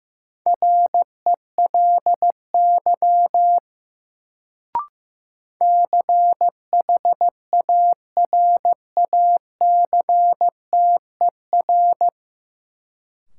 Morse code, heard as a keyed tone, about 15 wpm, 700 Hz. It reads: RELY CHARACTER
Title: Sentences from Top 1200 Words - New Word Drill - 15wpm